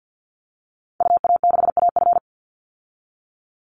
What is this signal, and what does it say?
Morse code, heard as a keyed tone, about 50 wpm, 700 Hz. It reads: UA6AF
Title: Call Signs Repeated 2x - 50wpm